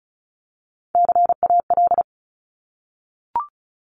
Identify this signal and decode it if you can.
Morse code, heard as a keyed tone, about 34 wpm, 700 Hz. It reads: CAL
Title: Top Three Letter Combinations in Words 35wpm